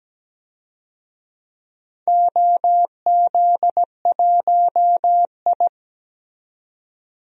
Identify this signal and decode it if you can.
Morse code, heard as a keyed tone, about 17 wpm, 700 Hz. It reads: OZ1I
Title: DXCC-100 Call Signs - 4 Characters - 17wpm